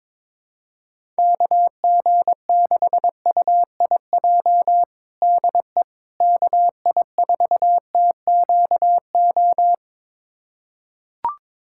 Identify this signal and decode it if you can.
Morse code, heard as a keyed tone, about 22 wpm, 700 Hz. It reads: KG6UIJ DE KI4TQO
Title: USA Callsign DE Callsign - 6 Characters - 22wpm